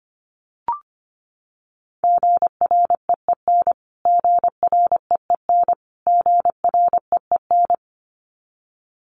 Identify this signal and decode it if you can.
Morse code, heard as a keyed tone, about 25 wpm, 700 Hz. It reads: GREEN GREEN GREEN
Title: Top 500 Words Repeated 25wpm